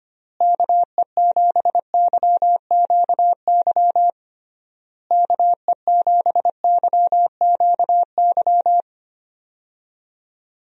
Morse code, heard as a keyed tone, about 25 wpm, 700 Hz. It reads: KE7YQY KE7YQY
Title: USA Call Signs - 6 Characters - Repeated 3x - 25wpm